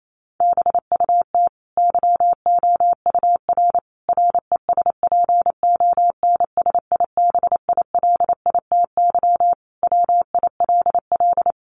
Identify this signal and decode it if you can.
Morse code, heard as a keyed tone, about 28 wpm, 700 Hz. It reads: BUT YOUR RESPONSIBILITY WILL
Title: Sentences from Top 1200 Words 28wpm